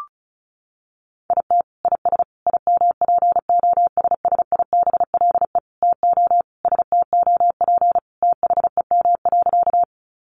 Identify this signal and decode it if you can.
Morse code, heard as a keyed tone, about 34 wpm, 700 Hz. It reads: IT IS IMPOSSIBLE TO STOP THEM.
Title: Sentences from Top 1100 Words 35wpm